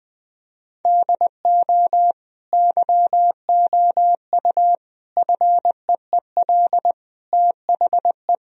Morse code, heard as a keyed tone, about 20 wpm, 700 Hz. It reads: DO YOU FEEL THE
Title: Sentences from Top 1000 Words - Review Part 3 - 20wpm